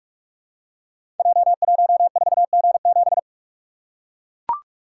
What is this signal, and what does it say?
Morse code, heard as a keyed tone, about 44 wpm, 700 Hz. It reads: J14G7
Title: Code Groups - 5 Letter:Numbers - 45wpm